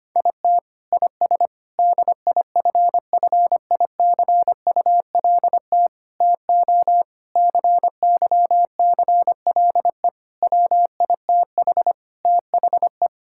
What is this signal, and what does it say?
Morse code, heard as a keyed tone, about 25 wpm, 700 Hz. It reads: IT IS DIFFICULT TO CYCLE WITH THE
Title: Sentences from Top 1000 Words - Review Part 6 - 25wpm